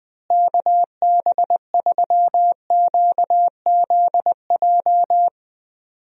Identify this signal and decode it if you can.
Morse code, heard as a keyed tone, about 20 wpm, 700 Hz. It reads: KB3QZJ
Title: USA Call Signs - 6 Characters - Repeated 3x - 20wpm